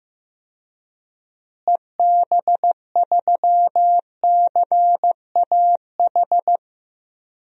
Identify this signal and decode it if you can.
Morse code, heard as a keyed tone, about 15 wpm, 700 Hz. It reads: EB3CAH
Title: DXCC-100 Call Signs - 6 Characters - Repeated 2x - 15wpm